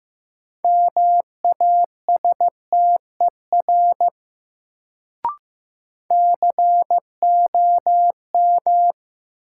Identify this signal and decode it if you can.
Morse code, heard as a keyed tone, about 15 wpm, 700 Hz. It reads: MASTER COM
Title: Sentences from Top 1300 Words - New Word Drill Alt - 15wpm